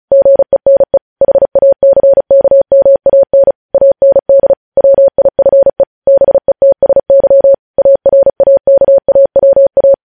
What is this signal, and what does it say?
Morse code, heard as a keyed tone, about 36 wpm, 550 Hz. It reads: GENE HACKMAN AND WIFE BETSY ARAKAWA
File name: podcast / News headlines-1x-35x35